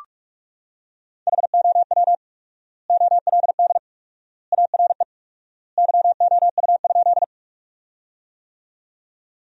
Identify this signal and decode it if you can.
Morse code, heard as a keyed tone, about 46 wpm, 700 Hz. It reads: HOW OLD ARE YOU?
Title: 4x Word-Spacing - Sentences from Top 200 Words - 45wpm